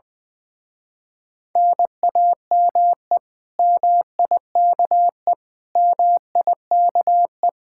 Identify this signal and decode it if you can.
Morse code, heard as a keyed tone, about 22 wpm, 700 Hz. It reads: NAME MIKE MIKE
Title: Speed-Racing Common QSO Phrases - 20wpm